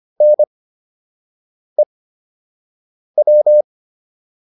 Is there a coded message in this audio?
Yes: Morse code from 0.2 to 3.6 s, heard as a keyed tone, about 25 wpm, spaced as for 6 wpm, 600 Hz. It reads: NEW